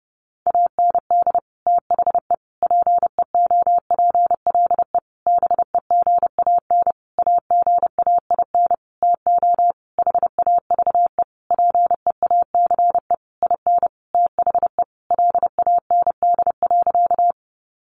Morse code, heard as a keyed tone, about 30 wpm, 700 Hz. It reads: AND THE PEOPLE BEGAN AGAIN TO HAVE PEACE IN THE LAND.
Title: Sentences from Top 1500 Words 30wpm